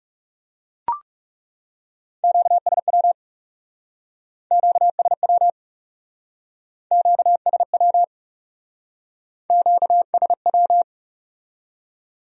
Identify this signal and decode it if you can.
Morse code, heard as a keyed tone, about 41 wpm, 700 Hz. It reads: QSW QSW QSW QSW